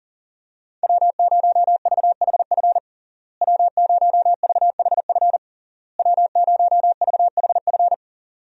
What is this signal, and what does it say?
Morse code, heard as a keyed tone, about 40 wpm, 690 Hz. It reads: W0VHF W0VHF W0VHF
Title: USA Call Signs - 5 Characters - Repeated 3x - 40wpm